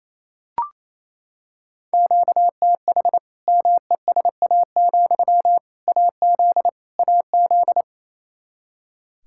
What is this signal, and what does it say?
Morse code, heard as a keyed tone, about 28 wpm, 700 Hz. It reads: QTH MESA, AZ AZ
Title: Practice QSO - Exchange 1 - Beginning - 28wpm